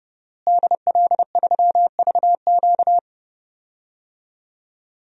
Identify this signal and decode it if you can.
Morse code, heard as a keyed tone, about 30 wpm, 700 Hz. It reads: DL3VQ
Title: DXCC-100 Call Signs - 5 Characters - Repeated 2x - 30wpm